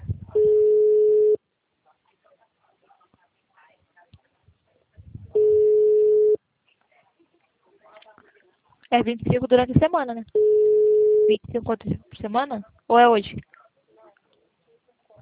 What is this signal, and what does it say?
Ringback tone (Europe, 425 Hz), with speech.